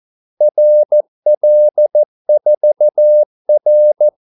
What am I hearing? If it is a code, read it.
Morse code, heard as a keyed tone, about 14 wpm, 600 Hz. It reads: RL4R
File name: Callsigns-14wpm